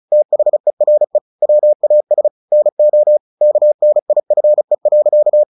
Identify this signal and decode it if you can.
Morse code, heard as a keyed tone, about 34 wpm, 600 Hz. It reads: THERE WAS NO KNIFE.
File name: the old man and the sea 2 35wpm